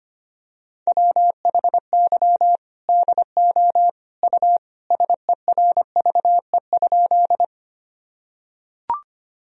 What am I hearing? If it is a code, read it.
Morse code, heard as a keyed tone, about 25 wpm, 700 Hz. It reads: WHY DO U SERVE?
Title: Sentences from Top 400 Words and Abbreviations 25wpm